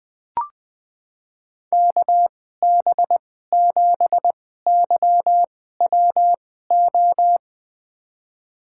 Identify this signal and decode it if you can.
Morse code, heard as a keyed tone, about 21 wpm, spaced as for 15 wpm, 700 Hz. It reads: KB7YWO
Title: USA Call Signs - 6 Characters - 2x Character Spacing - 20wpm